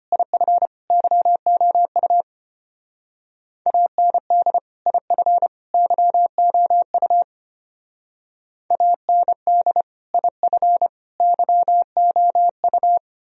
Morse code, heard as a keyed tone, about 31 wpm, 700 Hz. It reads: IF YOU AND IF YOU AND IF YOU
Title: Speed-Racing - Sets of 3 Words 25wpm